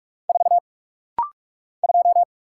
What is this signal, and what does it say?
Morse code, heard as a keyed tone, about 44 wpm, 700 Hz. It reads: 4 2
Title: Single Letter-Number - Rapid-Fire 45wpm